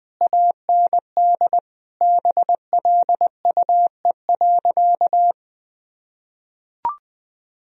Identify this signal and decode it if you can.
Morse code, heard as a keyed tone, about 21 wpm, 700 Hz. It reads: AND BLUE.
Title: Sentences from Top 1000 Words - Review Part 3 - 20wpm